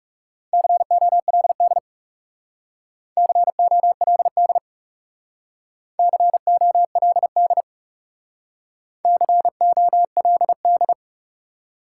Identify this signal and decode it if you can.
Morse code, heard as a keyed tone, about 38 wpm, 700 Hz. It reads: COLD COLD COLD COLD